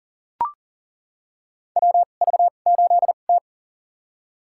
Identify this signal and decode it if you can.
Morse code, heard as a keyed tone, about 40 wpm, spaced as for 29 wpm, 700 Hz. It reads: WV8T
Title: USA Call Signs - 4 Characters - 2x Character Spacing - 40wpm